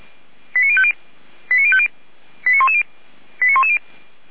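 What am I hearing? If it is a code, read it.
Five-tone selective call (ZVEI), heard as quick runs of five tones, tones of about 70 ms. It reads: 89050 89050 89110 89110
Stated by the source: ZVEI five-tone selective call received on SDR.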